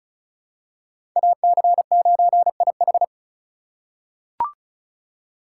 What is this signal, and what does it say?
Morse code, heard as a keyed tone, about 36 wpm, 700 Hz. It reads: AC9IH
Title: USA Call Signs - 6x Character Spacing - 35wpm